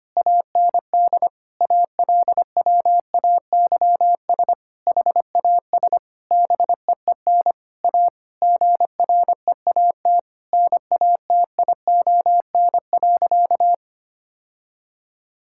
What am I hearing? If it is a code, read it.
Morse code, heard as a keyed tone, about 25 wpm, 700 Hz. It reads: AND ALWAYS HAS BEEN A GREAT NATION.
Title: Sentences from Top 1000 Words - Review Part 6 - 25wpm